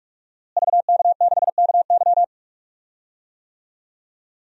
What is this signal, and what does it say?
Morse code, heard as a keyed tone, about 45 wpm, 700 Hz. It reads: VK6KY
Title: DXCC-100 Call Signs - 5 Characters - Repeated 2x - 45wpm